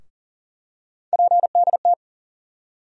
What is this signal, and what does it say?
Morse code, heard as a keyed tone, about 40 wpm, 700 Hz. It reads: PDT
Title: USA Call Sign Postfixes - 3 Letters 40wpm